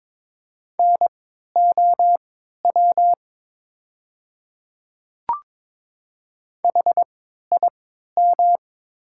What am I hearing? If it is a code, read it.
Morse code, heard as a keyed tone, about 22 wpm, spaced as for 12 wpm, 700 Hz. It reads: NOW HIM
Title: 3x Character Spacing - Top 100 Words - 22wpm